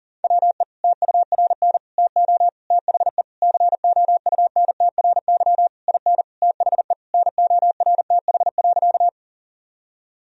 Morse code, heard as a keyed tone, about 37 wpm, 700 Hz. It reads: WE TURN TO THE COUNTRY IN THE NORTH.
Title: Sentences from Top 1000 Words - Review Part 1 - 40wpm